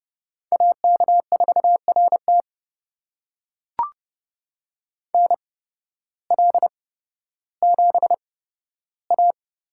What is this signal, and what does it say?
Morse code, heard as a keyed tone, about 30 wpm, spaced as for 14 wpm, 700 Hz. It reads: AK4RT NL7A
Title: USA Call Signs - 5 Characters - 8x Character Spacing - 30wpm